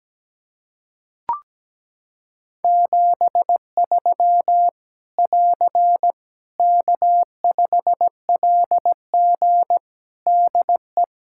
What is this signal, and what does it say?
Morse code, heard as a keyed tone, about 17 wpm, 700 Hz. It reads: 73 <AR> K5LG DE
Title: K1USN SST QSO Practice 17wpm